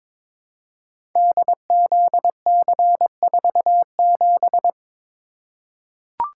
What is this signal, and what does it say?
Morse code, heard as a keyed tone, about 23 wpm, 700 Hz. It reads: DZC47
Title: Code Groups - 5 Letter:Numbers - 22wpm